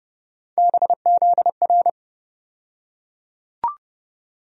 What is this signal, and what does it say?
Morse code, heard as a keyed tone, about 30 wpm, 700 Hz. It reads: BZR
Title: USA Call Sign Postfixes - 3 Letters 30wpm